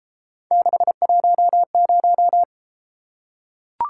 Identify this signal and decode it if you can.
Morse code, heard as a keyed tone, about 33 wpm, 700 Hz. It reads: 610